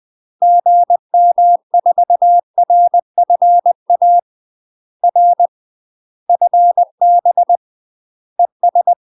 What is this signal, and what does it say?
Morse code, heard as a keyed tone, about 20 wpm, 700 Hz. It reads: GM4RFA R FB ES
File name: UK MASTER QSOs TRAINING / 20WPM / UK MASTER QSOs 0059